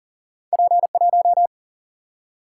Morse code, heard as a keyed tone, about 40 wpm, 700 Hz. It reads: P 1
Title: Single-Letter Single-Number 40wpm